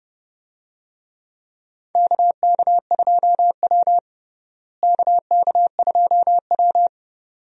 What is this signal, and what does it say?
Morse code, heard as a keyed tone, about 32 wpm, 700 Hz. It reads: KK2W KK2W
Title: USA Call Signs - 4 Characters - Repeated 2x - 30wpm